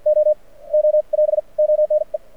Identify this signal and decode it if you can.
Morse code, heard as a keyed tone, about 56 wpm, spaced as for 32 wpm, 600 Hz. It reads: OO <AS> ONE
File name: archived / CRSnet